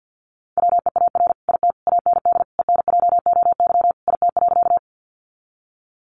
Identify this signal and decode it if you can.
Morse code, heard as a keyed tone, about 50 wpm, 700 Hz. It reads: WEAR IT AND ENJOY IT.
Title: Sentences from Top 1500 Words 50wpm